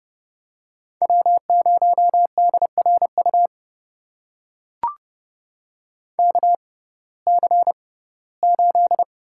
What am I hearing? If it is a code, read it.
Morse code, heard as a keyed tone, about 29 wpm, spaced as for 18 wpm, 700 Hz. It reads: W0DRU KC8
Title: USA Call Signs - 6x Character Spacing - 30wpm